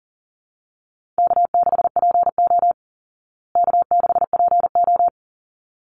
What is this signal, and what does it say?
Morse code, heard as a keyed tone, about 40 wpm, 700 Hz. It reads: K6PO K6PO
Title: Easy Call Signs Repeated 2x - 40wpm